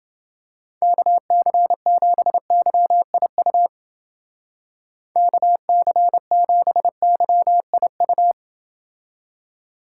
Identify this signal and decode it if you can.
Morse code, heard as a keyed tone, about 28 wpm, 700 Hz. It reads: KC7YIU KC7YIU